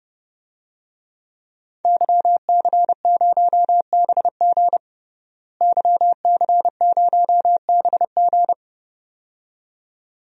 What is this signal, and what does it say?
Morse code, heard as a keyed tone, about 30 wpm, 700 Hz. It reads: YC0BG YC0BG
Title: DXCC-100 Call Signs - 5 Characters - Repeated 2x - 30wpm